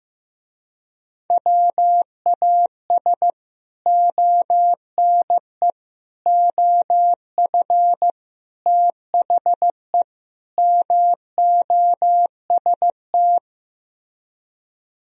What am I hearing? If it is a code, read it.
Morse code, heard as a keyed tone, about 16 wpm, 700 Hz. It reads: WAS ONE OF THE MOST
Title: Speed-Racing - Sets of 5 Words 15wpm